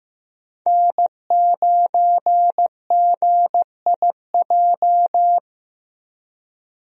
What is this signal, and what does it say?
Morse code, heard as a keyed tone, about 15 wpm, 700 Hz. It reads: N9GIJ
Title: USA Call Signs - 5 Characters - Repeated 2x - 15wpm